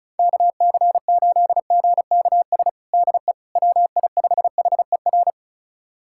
Morse code, heard as a keyed tone, about 34 wpm, 700 Hz. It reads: KC8GKS DE WI5HER
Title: USA Callsign DE Callsign 35wpm